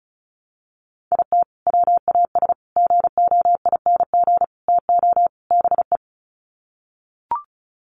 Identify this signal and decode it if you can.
Morse code, heard as a keyed tone, about 34 wpm, 700 Hz. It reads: IT WAS GOING TO BE
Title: Sets of 5 Words 35wpm